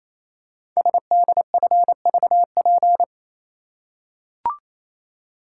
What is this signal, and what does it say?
Morse code, heard as a keyed tone, about 27 wpm, 700 Hz. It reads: SDFVP
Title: Code Groups - 5 Letter:Numbers - 28wpm